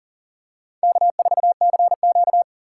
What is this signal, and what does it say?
Morse code, heard as a keyed tone, about 40 wpm, 700 Hz. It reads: K4CQ
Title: USA Call Signs - 4 Characters - 4x Character Spacing - 40wpm